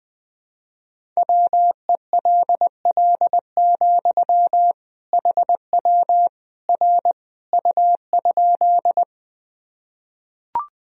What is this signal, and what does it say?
Morse code, heard as a keyed tone, about 20 wpm, 700 Hz. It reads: WELL, HW R U?
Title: Sentences from Top 200 Words and Abbreviations 20wpm